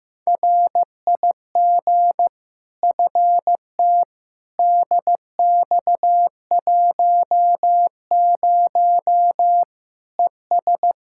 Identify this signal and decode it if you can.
Morse code, heard as a keyed tone, about 15 wpm, 700 Hz. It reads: RIG FT DX10 ES
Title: Practice QSO - Exchange 2 - Info - 15wpm